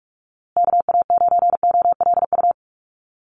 Morse code, heard as a keyed tone, about 44 wpm, 700 Hz. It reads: KA9OLU
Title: USA Call Signs Repeated 2x 45wpm